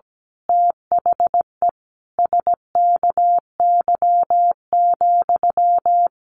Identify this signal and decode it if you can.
Morse code, heard as a keyed tone, about 17 wpm, 700 Hz. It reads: THE SKY,
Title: A Sound Of Thunder 17wpm